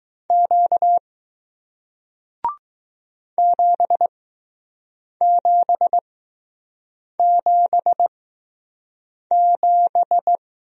Morse code, heard as a keyed tone, about 20 wpm, spaced as for 6 wpm, 700 Hz. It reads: Q 7777